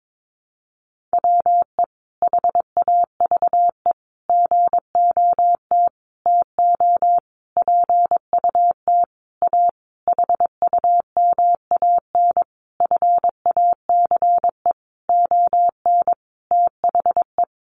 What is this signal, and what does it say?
Morse code, heard as a keyed tone, about 22 wpm, 700 Hz. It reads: WE HAVE GOT TO PUT A HUMAN FACE ON THE
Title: Sentences from Top 1200 Words 22wpm